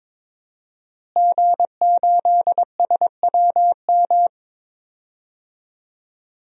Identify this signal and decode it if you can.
Morse code, heard as a keyed tone, about 22 wpm, 700 Hz. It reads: G8SWM
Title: DXCC-100 Call Signs - 5 Characters - Repeated 3x - 22wpm